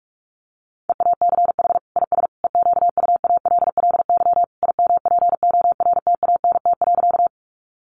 Morse code, heard as a keyed tone, about 44 wpm, 700 Hz. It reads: EACH IS EQUALLY IMPORTANT.
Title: Sentences from Top 1400 Words 45wpm